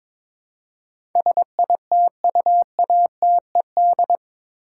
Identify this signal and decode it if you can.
Morse code, heard as a keyed tone, about 22 wpm, 700 Hz. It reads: SITUATED
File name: practice / 22wpm / Eight Letter Words - 22wpm